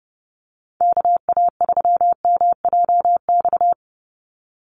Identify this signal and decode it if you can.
Morse code, heard as a keyed tone, about 30 wpm, 700 Hz. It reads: KA3MJX
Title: USA Call Signs 30wpm